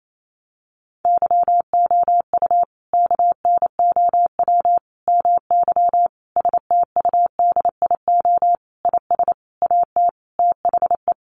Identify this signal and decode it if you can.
Morse code, heard as a keyed tone, about 28 wpm, 700 Hz. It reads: YOU KNOW MY STUDIO IS AT THE
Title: Sentences from Top 1200 Words 28wpm